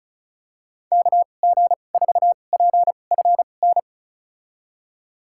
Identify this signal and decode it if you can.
Morse code, heard as a keyed tone, about 36 wpm, spaced as for 25 wpm, 700 Hz. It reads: KG4PFN
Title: USA Call Signs - 2x Character Spacing - 35wpm